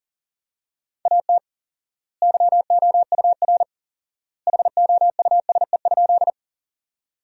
Morse code, heard as a keyed tone, about 37 wpm, 700 Hz. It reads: AT YOUR HOUSE?
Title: 4x Word-Spacing - Sentences from Top 200 Words - 40wpm